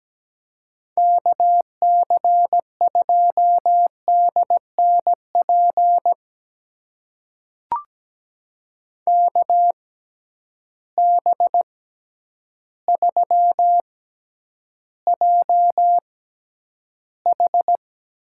Morse code, heard as a keyed tone, about 17 wpm, spaced as for 9 wpm, 700 Hz. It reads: KC2DNP KB3JH
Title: USA Call Signs - 6 Characters - 6x Character Spacing - 17wpm